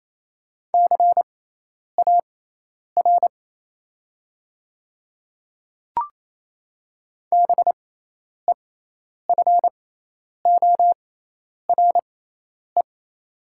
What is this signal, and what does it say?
Morse code, heard as a keyed tone, about 28 wpm, spaced as for 10 wpm, 700 Hz. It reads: CAR BEFORE
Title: 6x Character Spacing - Top 300 Words - 28wpm